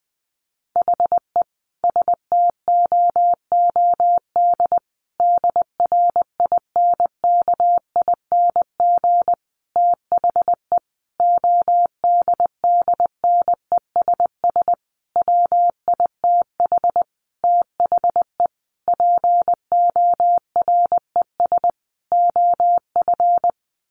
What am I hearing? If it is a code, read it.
Morse code, heard as a keyed tone, about 19 wpm, 700 Hz. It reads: HE STOOD DRINKING THE ODDNESS WITH THE PORES OF
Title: A Sound Of Thunder 20wpm